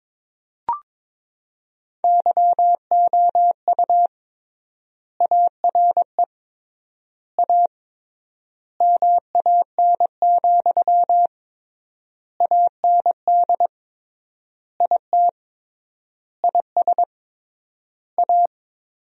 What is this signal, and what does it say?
Morse code, heard as a keyed tone, about 22 wpm, 700 Hz. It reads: YOU ARE A MAN, AND IT IS A
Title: 3x Word-Spacing - Sentences from Top 200 Words - 22wpm